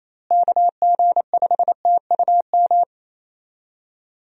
Morse code, heard as a keyed tone, about 28 wpm, 700 Hz. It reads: KG5TUM
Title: USA Call Signs - 6 Characters - Repeated 3x - 28wpm